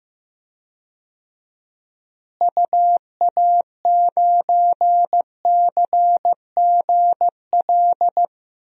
Morse code, heard as a keyed tone, about 15 wpm, 700 Hz. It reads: UA9CGL
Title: DXCC-100 Call Signs - 6 Characters - Repeated 3x - 15wpm